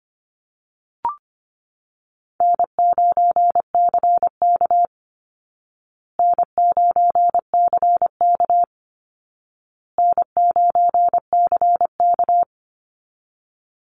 Morse code, heard as a keyed tone, about 25 wpm, 700 Hz. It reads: N9CK N9CK N9CK
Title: Easy Call Signs Repeated 3x - 25wpm